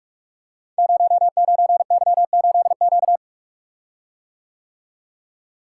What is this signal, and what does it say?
Morse code, heard as a keyed tone, about 44 wpm, 690 Hz. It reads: 09Y8Q
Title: Code Groups - 5 Letter:Numbers - 45wpm